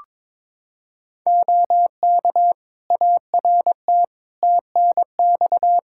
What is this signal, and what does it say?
Morse code, heard as a keyed tone, about 22 wpm, 700 Hz. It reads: OK ART TNX
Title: Practice QSO - Exchange 4 - Ending - 22wpm